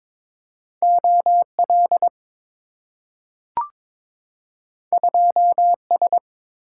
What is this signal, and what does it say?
Morse code, heard as a keyed tone, about 23 wpm, 690 Hz. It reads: OL 2S
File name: practice / 22wpm / Two Letter-Number Combinations - 22wpm